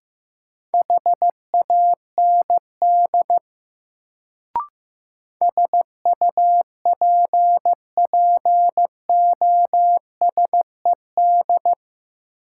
Morse code, heard as a keyed tone, about 15 wpm, 700 Hz. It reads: HAND SUPPOSED
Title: Sets of 3 Words - New Word Drill - 15wpm